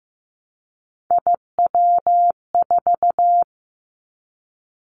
Morse code, heard as a keyed tone, about 15 wpm, 700 Hz. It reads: IW4
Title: Callsign Prefixes for Top 50 DXCC - 15wpm